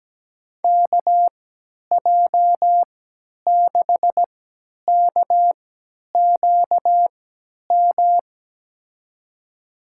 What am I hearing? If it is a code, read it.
Morse code, heard as a keyed tone, about 17 wpm, spaced as for 10 wpm, 700 Hz. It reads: KJ6KQM
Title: USA Call Signs - 6 Characters - 3x Character Spacing - 17wpm